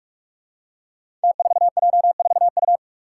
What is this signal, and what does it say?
Morse code, heard as a keyed tone, about 44 wpm, 700 Hz. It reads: T4J4U
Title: Five Challenging Letter-Number Combinations 45wpm